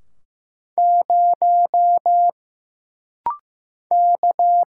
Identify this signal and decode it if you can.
Morse code, heard as a keyed tone, about 15 wpm, 700 Hz. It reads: 0 K
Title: Single Letter-Number Alt - Rapid-Fire 15wpm